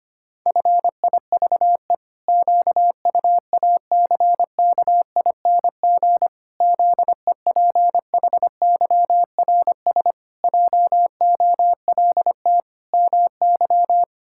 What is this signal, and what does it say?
Morse code, heard as a keyed tone, about 25 wpm, 700 Hz. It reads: FIVE QUACKING ZEPHYRS JOLT MY
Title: Pangram Challenge v2 - 25wpm